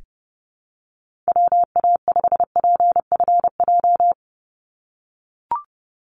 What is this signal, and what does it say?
Morse code, heard as a keyed tone, about 30 wpm, 700 Hz. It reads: WA5PFJ
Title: Call Signs - 30wpm